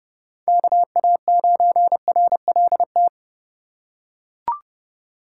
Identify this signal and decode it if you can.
Morse code, heard as a keyed tone, about 30 wpm, 700 Hz. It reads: KA9RLT
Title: USA Call Signs - 6 Characters - 2x Character Spacing - 30wpm